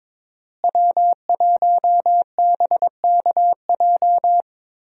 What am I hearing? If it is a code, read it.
Morse code, heard as a keyed tone, about 22 wpm, 700 Hz. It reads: W1BKJ
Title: USA Call Signs - 5 Characters - Repeated 2x - 22wpm